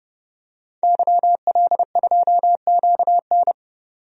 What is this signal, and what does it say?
Morse code, heard as a keyed tone, about 31 wpm, 700 Hz. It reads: YL2QN
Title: DXCC-100 Call Signs - 5 Characters - 30wpm